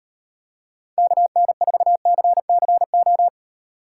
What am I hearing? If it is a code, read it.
Morse code, heard as a keyed tone, about 38 wpm, 700 Hz. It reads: KN4CCO